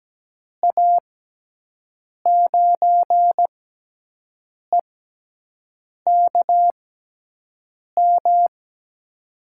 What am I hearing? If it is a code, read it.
Morse code, heard as a keyed tone, about 17 wpm, spaced as for 6 wpm, 700 Hz. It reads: A9EKM
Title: USA Call Signs - 6 Characters - 6x Character Spacing - 17wpm